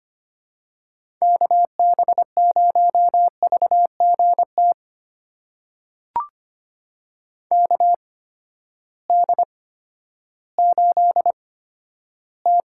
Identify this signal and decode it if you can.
Morse code, heard as a keyed tone, about 25 wpm, spaced as for 13 wpm, 700 Hz. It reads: KB0VGT KD8T
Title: USA Call Signs - 6 Characters - 8x Character Spacing - 25wpm